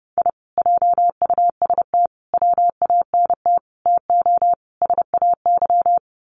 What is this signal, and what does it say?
Morse code, heard as a keyed tone, about 30 wpm, 700 Hz. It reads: I JUST WANT TO SAY
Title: Sets of 5 Words Encore 30wpm